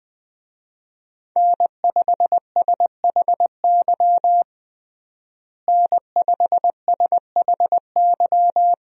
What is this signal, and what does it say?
Morse code, heard as a keyed tone, about 20 wpm, 700 Hz. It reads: N5SHY N5SHY
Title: USA Call Signs - 5 Characters - Repeated 2x - 20wpm